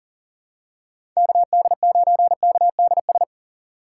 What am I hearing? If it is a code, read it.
Morse code, heard as a keyed tone, about 39 wpm, 690 Hz. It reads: KD9KDS
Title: USA Call Signs - 4x Character Spacing - 40wpm